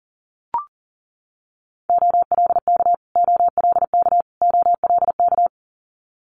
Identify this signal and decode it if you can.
Morse code, heard as a keyed tone, about 40 wpm, 700 Hz. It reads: OLK OLK OLK
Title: Multi-letter Phonemes Repeated 3x 40wpm